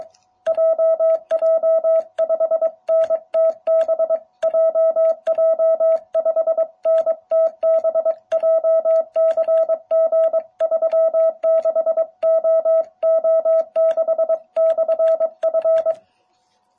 Morse code, heard as a keyed tone, about 22 wpm, 650 Hz. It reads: JJ5NTB JJ5NTB JCG36OO6/F